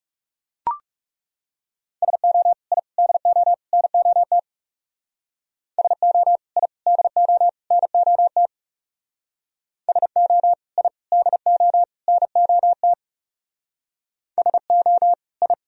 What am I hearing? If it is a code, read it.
Morse code, heard as a keyed tone, about 38 wpm, 700 Hz. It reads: SO I DO NOT SO I DO NOT SO I DO NOT SO I